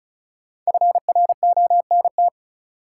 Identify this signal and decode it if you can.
Morse code, heard as a keyed tone, about 36 wpm, 700 Hz. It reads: FRONT
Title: Sentences from Top 500 Words - New Word Drill - 35wpm